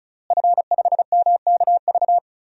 Morse code, heard as a keyed tone, about 36 wpm, 700 Hz. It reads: F5MKV
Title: DXCC-100 Call Signs - 5 Characters - Repeated 3x - 35wpm